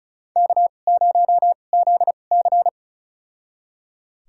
Morse code, heard as a keyed tone, about 34 wpm, spaced as for 25 wpm, 700 Hz. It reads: K0ZC
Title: USA Call Signs - 4 Characters - 2x Character Spacing - 35wpm